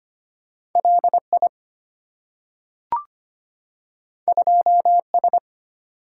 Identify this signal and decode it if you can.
Morse code, heard as a keyed tone, about 25 wpm, 700 Hz. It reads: LI 2S
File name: Two Letter-Number Combinations - 25wpm